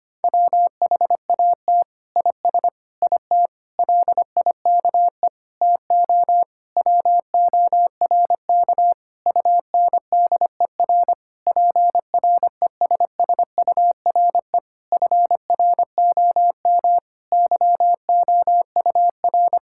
Morse code, heard as a keyed tone, about 25 wpm, 700 Hz. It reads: WHAT IS IT LIKE TO WORK UNDER PRESSURE FROM YOUR
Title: Sentences from Top 1000 Words - Review Part 6 - 25wpm